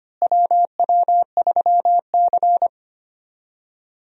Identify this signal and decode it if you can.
Morse code, heard as a keyed tone, about 25 wpm, 700 Hz. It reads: WW3C
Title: USA Call Signs - 4 Characters - Repeated 3x - 25wpm